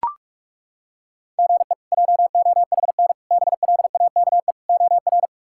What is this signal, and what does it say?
Morse code, heard as a keyed tone, about 41 wpm, 700 Hz. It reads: GE JOHN BLAKE OR
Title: K1USN SST QSO Practice - 45wpm